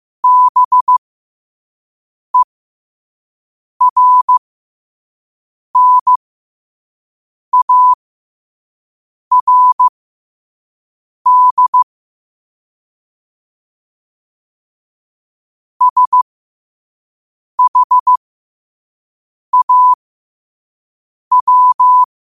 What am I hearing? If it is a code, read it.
Morse code, heard as a keyed tone, about 15 wpm, spaced as for 5 wpm, 1000 Hz. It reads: BERNARD SHAW
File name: qotdfiles / 5wpm